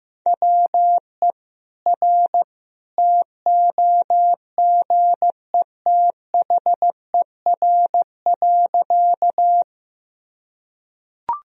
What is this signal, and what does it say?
Morse code, heard as a keyed tone, about 15 wpm, 700 Hz. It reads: WE R TOGETHER.
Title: Sentences from Top 300 Words and Abbreviations 15wpm